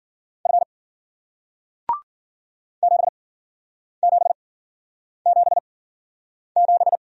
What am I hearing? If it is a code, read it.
Morse code, heard as a keyed tone, about 56 wpm, spaced as for 9 wpm, 700 Hz. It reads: F 7777